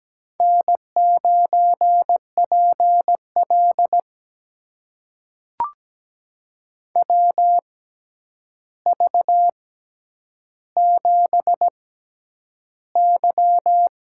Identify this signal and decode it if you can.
Morse code, heard as a keyed tone, about 17 wpm, spaced as for 9 wpm, 700 Hz. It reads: N9PL WV7Y
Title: USA Call Signs - 4 Characters - 6x Character Spacing - 17wpm